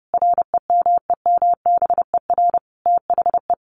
Morse code, heard as a keyed tone, about 30 wpm, 700 Hz. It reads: REMEMBER THE
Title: Sentences from Top 800 Words 30wpm